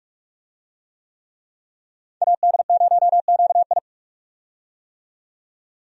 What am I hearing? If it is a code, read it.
Morse code, heard as a keyed tone, about 42 wpm, 700 Hz. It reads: AD0QI
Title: USA Call Signs - 5 Characters - 45wpm